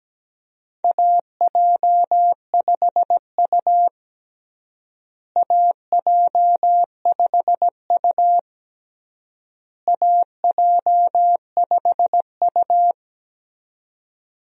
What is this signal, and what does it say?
Morse code, heard as a keyed tone, about 17 wpm, 700 Hz. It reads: AJ5U AJ5U AJ5U
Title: USA Call Signs - 4 Characters - Repeated 3x - 17wpm